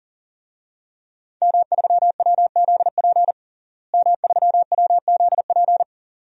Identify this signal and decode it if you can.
Morse code, heard as a keyed tone, about 40 wpm, 700 Hz. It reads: M3WZP M3WZP
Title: DXCC-100 Call Signs - 5 Characters - Repeated 2x - 40wpm